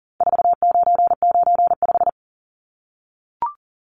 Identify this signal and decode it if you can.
Morse code, heard as a keyed tone, about 40 wpm, 700 Hz. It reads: 4995
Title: Numbers - 4-Digit 40wpm